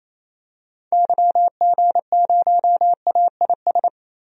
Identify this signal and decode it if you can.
Morse code, heard as a keyed tone, about 28 wpm, 700 Hz. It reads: YG0AIS
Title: DXCC-100 Call Signs - 6 Characters - Repeated 3x - 28wpm